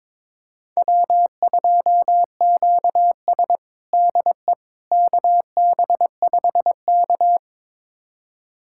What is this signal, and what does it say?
Morse code, heard as a keyed tone, about 22 wpm, 700 Hz. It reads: W2QS DE KB5K
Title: USA Callsign DE Callsign - 4 Characters - 22wpm